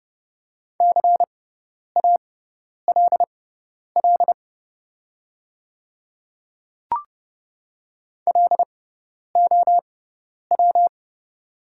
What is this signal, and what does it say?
Morse code, heard as a keyed tone, about 32 wpm, spaced as for 10 wpm, 700 Hz. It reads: CALL LOW
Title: 6x Character Spacing - Top 200 Words - 30wpm